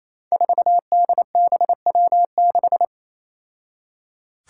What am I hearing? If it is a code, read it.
Morse code, heard as a keyed tone, about 27 wpm, 700 Hz. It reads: 4DBW6
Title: Five Challenging Letter-Number Combinations 28wpm